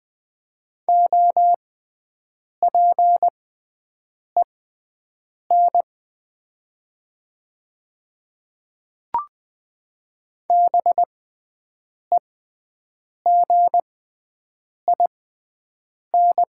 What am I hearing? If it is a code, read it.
Morse code, heard as a keyed tone, about 20 wpm, spaced as for 7 wpm, 700 Hz. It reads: OPEN BEGIN
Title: 6x Character Spacing - Top 300 Words - 20wpm